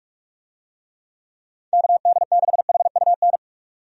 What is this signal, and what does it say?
Morse code, heard as a keyed tone, about 46 wpm, 700 Hz. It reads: KD6HUN
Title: USA Call Signs - 6 Characters - 45wpm